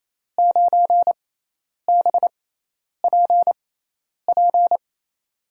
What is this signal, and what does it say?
Morse code, heard as a keyed tone, about 28 wpm, spaced as for 10 wpm, 700 Hz. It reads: 9BPP
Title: USA Call Signs - 6 Characters - 6x Character Spacing - 28wpm